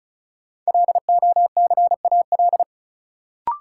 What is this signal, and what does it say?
Morse code, heard as a keyed tone, about 34 wpm, 700 Hz. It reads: LOCAL
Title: Sentences from Top 1500 Words - New Word Drill Alt - 35wpm